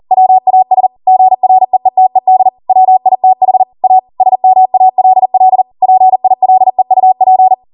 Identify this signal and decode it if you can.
Morse code, heard as a keyed tone, about 40 wpm, 750 Hz. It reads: WAS GREETED WITH A SMALL PILEUP